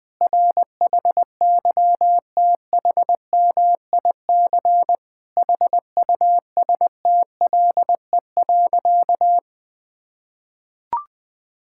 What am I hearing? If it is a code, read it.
Morse code, heard as a keyed tone, about 20 wpm, 700 Hz. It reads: RHYTHMIC HUSTLE.
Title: Difficult Sentences with Hints - 20wpm